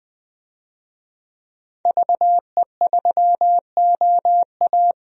Morse code, heard as a keyed tone, about 20 wpm, 700 Hz. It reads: VE3OA
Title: DXCC-100 Call Signs - 5 Characters - 20wpm